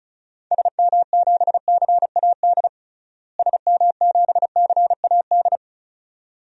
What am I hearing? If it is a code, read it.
Morse code, heard as a keyed tone, about 34 wpm, 700 Hz. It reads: SM7CAD SM7CAD
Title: DXCC-100 Call Signs - 6 Characters - Repeated 2x - 35wpm